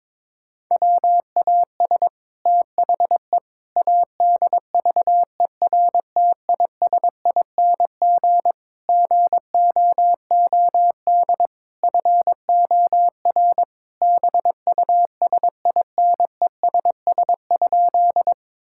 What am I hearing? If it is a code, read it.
Morse code, heard as a keyed tone, about 22 wpm, 700 Hz. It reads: WAS THE ADVERTISING GOOD FOR BUSINESS?
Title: Sentences from Top 1000 Words - Review Part 6 - 22wpm